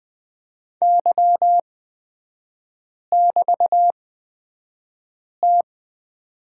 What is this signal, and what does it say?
Morse code, heard as a keyed tone, about 20 wpm, spaced as for 5 wpm, 700 Hz. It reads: Y<BT>T
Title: 255 Lesson 41e - Period - Three Characters 20wpm